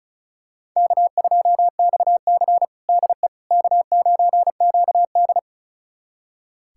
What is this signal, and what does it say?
Morse code, heard as a keyed tone, about 36 wpm, 700 Hz. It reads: K2XC DE K9QD
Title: USA Callsign DE Callsign - 4 Characters - 35wpm